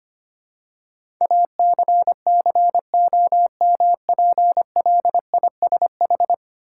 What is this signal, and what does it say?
Morse code, heard as a keyed tone, about 25 wpm, 700 Hz. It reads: ACCOMPLISH
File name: practice / 25wpm / Ten Letter Words - 25wpm